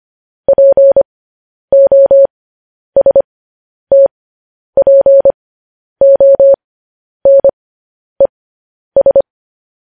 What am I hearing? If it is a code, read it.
Morse code, heard as a keyed tone, about 25 wpm, spaced as for 10 wpm, 550 Hz. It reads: POSTPONES
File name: podcast / News headlines-1x-25x10